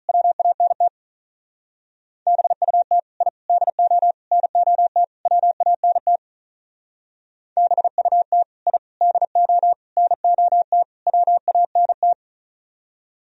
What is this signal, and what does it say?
Morse code, heard as a keyed tone, about 43 wpm, 700 Hz. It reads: WANT BUT I DO NOT WANT BUT I DO NOT WANT
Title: Speed-Racing - Sets of 5 Words 35wpm